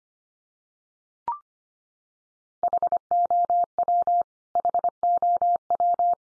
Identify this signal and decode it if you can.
Morse code, heard as a keyed tone, about 25 wpm, 700 Hz. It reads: HOW HOW
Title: Top 100 Words 25wpm x3